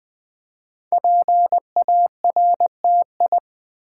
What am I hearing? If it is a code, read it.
Morse code, heard as a keyed tone, about 20 wpm, 700 Hz. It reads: PARTI
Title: Build A Word - 5 Letters At A Time - 10-Letter Words - 20wpm